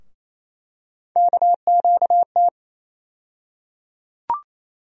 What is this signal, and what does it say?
Morse code, heard as a keyed tone, about 28 wpm, 700 Hz. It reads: KQT
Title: USA Call Sign Postfixes - 3 Letters 28wpm